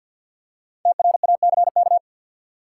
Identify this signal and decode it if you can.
Morse code, heard as a keyed tone, about 51 wpm, 700 Hz. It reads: TRACK
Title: Five Letter Words - 50wpm